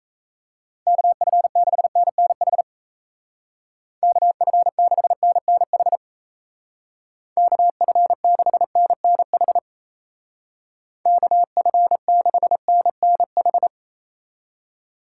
Morse code, heard as a keyed tone, about 34 wpm, 700 Hz. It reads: KF6NNH KF6NNH KF6NNH KF6NNH
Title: Speed Racing - USA 2x3 Call Signs 28wpm